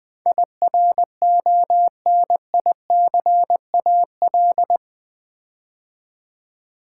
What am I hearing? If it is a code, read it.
Morse code, heard as a keyed tone, about 20 wpm, 700 Hz. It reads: IRONICAL
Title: Build A Word - 4 Letters At A Time - 10-Letter Words - 20wpm